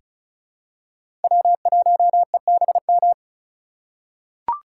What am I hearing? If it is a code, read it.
Morse code, heard as a keyed tone, about 36 wpm, 700 Hz. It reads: W1EBM
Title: K1USN SST QSO Practice - 35wpm